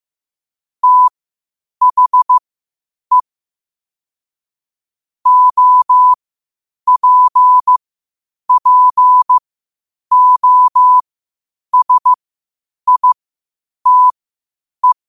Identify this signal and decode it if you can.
Morse code, heard as a keyed tone, about 15 wpm, spaced as for 8 wpm, 1000 Hz. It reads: THE OPPOSITE
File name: qotdfiles / 8wpm